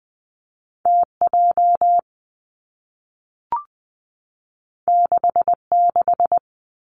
Morse code, heard as a keyed tone, about 20 wpm, 700 Hz. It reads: T J 6 6
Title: Two Challenging Letter-Number Combinations 20wpm